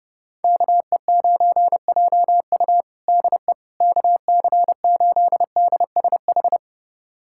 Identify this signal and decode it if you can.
Morse code, heard as a keyed tone, about 30 wpm, 700 Hz. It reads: KE9JU DE KC8DSH
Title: USA Callsign DE Callsign 30wpm